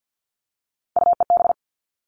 Morse code, heard as a keyed tone, about 48 wpm, 700 Hz. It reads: UEB
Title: All Cut Numbers - 3-Digit 50wpm